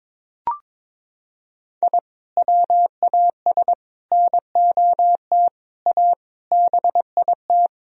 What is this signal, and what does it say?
Morse code, heard as a keyed tone, about 21 wpm, 700 Hz. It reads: I WAS NOT A BIT
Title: Sentences from Top 1000 Words - Review Part 4 - 22wpm